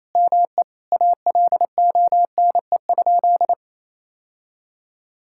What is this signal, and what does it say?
Morse code, heard as a keyed tone, about 28 wpm, 700 Hz. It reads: ME ALONE?
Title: Sentences from Top 1600 Words 28wpm